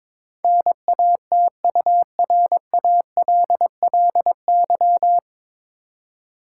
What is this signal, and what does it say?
Morse code, heard as a keyed tone, about 22 wpm, 700 Hz. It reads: NATURALLY
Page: Morse Code Ninja practice 22wpm